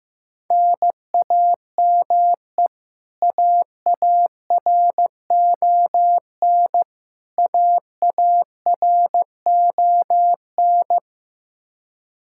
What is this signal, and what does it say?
Morse code, heard as a keyed tone, about 15 wpm, 700 Hz. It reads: NAME AARON AARON
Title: Practice QSO - Exchange 1 - Beginning - 15wpm